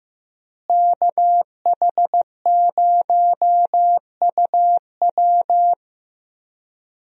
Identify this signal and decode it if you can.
Morse code, heard as a keyed tone, about 15 wpm, 700 Hz. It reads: KH0UW
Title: USA Call Signs - 5 Characters - Repeated 3x - 15wpm